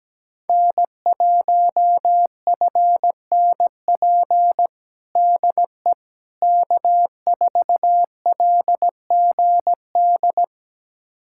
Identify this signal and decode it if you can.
Morse code, heard as a keyed tone, about 17 wpm, 700 Hz. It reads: N1FNP DE K4LGD
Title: USA Callsign DE Callsign - 5 Characters - 17wpm